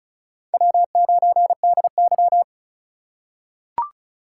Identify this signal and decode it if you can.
Morse code, heard as a keyed tone, about 34 wpm, 700 Hz. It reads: W9DY
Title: K1USN SST QSO Practice - 35wpm